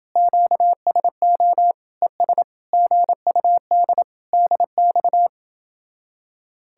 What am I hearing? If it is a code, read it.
Morse code, heard as a keyed tone, about 30 wpm, 700 Hz. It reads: QSO ES GUD DX